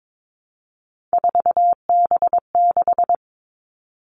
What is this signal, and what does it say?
Morse code, heard as a keyed tone, about 22 wpm, 700 Hz. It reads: 4B6
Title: Three Challenging Letter-Number Combinations 22wpm